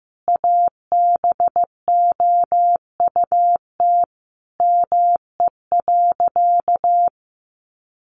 Morse code, heard as a keyed tone, about 15 wpm, 700 Hz. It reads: ABOUT ME.
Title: Sentences from Top 1200 Words 15wpm